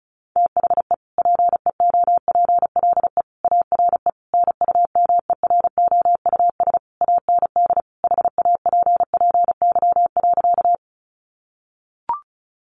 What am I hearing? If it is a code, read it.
Morse code, heard as a keyed tone, about 34 wpm, 700 Hz. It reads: THE PEOPLE ARE NUMEROUS AND HAPPY.
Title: Sentences from Top 1400 Words 35wpm